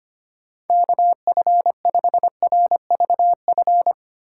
Morse code, heard as a keyed tone, about 25 wpm, 700 Hz. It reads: KF5RVF
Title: USA Call Signs - 6 Characters - 4x Character Spacing - 25wpm